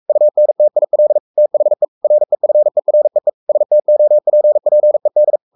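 Morse code, heard as a keyed tone, about 44 wpm, 600 Hz. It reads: UNTIL THE REFEREE STOPPED
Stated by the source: Morse code at 43 wpm sending Ken Follett's novel Never in English, from CW med Gnister.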